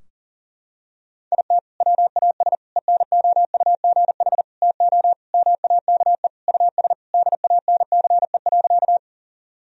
Morse code, heard as a keyed tone, about 40 wpm, 700 Hz. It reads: IT WAS ENOUGH TO MAKE US DANCE.
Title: Sentences from Top 1000 Words - Review Part 4 - 40wpm